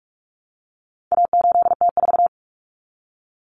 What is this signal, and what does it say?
Morse code, heard as a keyed tone, about 44 wpm, 700 Hz. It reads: A8T4
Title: Common Cut Numbers - 4-Digit 45wpm